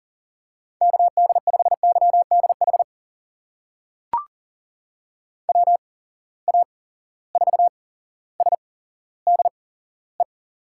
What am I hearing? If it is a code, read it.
Morse code, heard as a keyed tone, about 40 wpm, spaced as for 17 wpm, 700 Hz. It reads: KD5YDH WA4SDE
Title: USA Call Signs - 8x Character Spacing - 40wpm